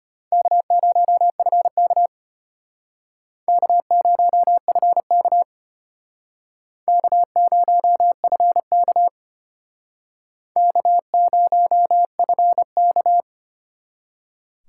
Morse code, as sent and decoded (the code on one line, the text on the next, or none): -.- ----- ..-. -.- / -.- ----- ..-. -.- / -.- ----- ..-. -.- / -.- ----- ..-. -.-
K0FK K0FK K0FK K0FK